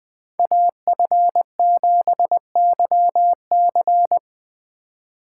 Morse code, heard as a keyed tone, about 23 wpm, 700 Hz. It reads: AF7YC
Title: Speed Racing - USA 2x2 Call Signs 20wpm